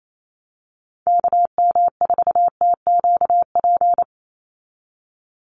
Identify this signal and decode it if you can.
Morse code, heard as a keyed tone, about 28 wpm, 700 Hz. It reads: KM4TQP
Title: USA Call Signs 28wpm